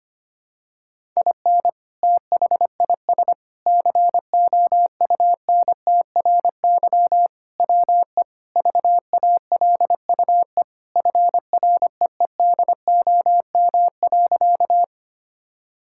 Morse code, heard as a keyed tone, about 25 wpm, 700 Hz. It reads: IN THIS COUNTRY WE VALUE FREEDOM.
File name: practice / 25wpm / Sentences with 3-Word Phrases - 25wpm